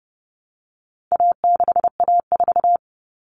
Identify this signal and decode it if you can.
Morse code, heard as a keyed tone, about 30 wpm, 700 Hz. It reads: A6A4
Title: Common Cut Numbers - 4-Digit 30wpm